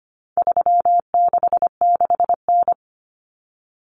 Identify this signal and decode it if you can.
Morse code, heard as a keyed tone, about 25 wpm, 700 Hz. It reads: 366N
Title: Common Cut Numbers - 4-Digit 25wpm